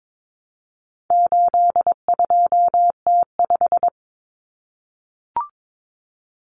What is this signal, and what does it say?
Morse code, heard as a keyed tone, about 22 wpm, 700 Hz. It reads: 82T5
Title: Common Cut Numbers - 4-Digit 22wpm